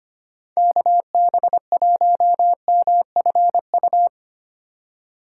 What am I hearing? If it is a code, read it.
Morse code, heard as a keyed tone, about 25 wpm, 700 Hz. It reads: KB1MFU
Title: USA Call Signs - 6 Characters - 3x Character Spacing - 25wpm